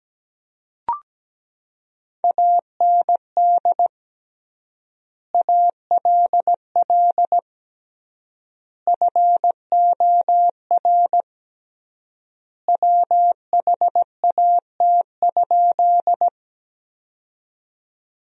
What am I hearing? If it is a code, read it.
Morse code, heard as a keyed tone, about 17 wpm, 700 Hz. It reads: AND ALL FOR WHAT?
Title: 3x Word-Spacing - Sentences from Top 100 Words - 17wpm